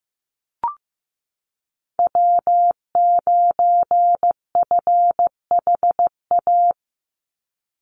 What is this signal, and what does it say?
Morse code, heard as a keyed tone, about 15 wpm, 700 Hz. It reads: W9FHA
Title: NA SOTA Chaser Callsigns 15wpm